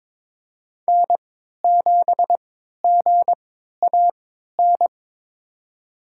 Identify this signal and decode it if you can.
Morse code, heard as a keyed tone, about 22 wpm, spaced as for 13 wpm, 700 Hz. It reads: N7GAN
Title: USA Call Signs - 5 Characters - 3x Character Spacing - 22wpm